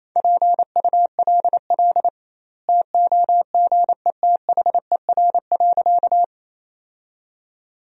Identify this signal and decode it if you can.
Morse code, heard as a keyed tone, about 28 wpm, 700 Hz. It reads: PULL TOGETHER.
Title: 2x Word-Spacing - Sentences from Top 400 Words - 28wpm